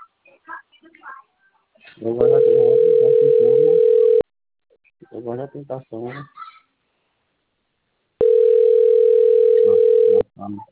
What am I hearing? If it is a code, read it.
Ringback tone (North America), 440 + 480 Hz, with speech.
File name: rg-1600-unknown- - - .75345